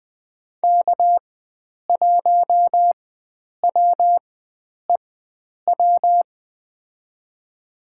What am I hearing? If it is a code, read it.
Morse code, heard as a keyed tone, about 20 wpm, spaced as for 9 wpm, 700 Hz. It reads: K1WEW
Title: USA Call Signs - 4x Character Spacing - 20wpm